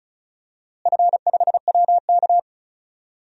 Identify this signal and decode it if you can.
Morse code, heard as a keyed tone, about 34 wpm, 700 Hz. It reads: F5WK
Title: DXCC-100 Call Signs - 4 Characters - Repeated 2x - 35wpm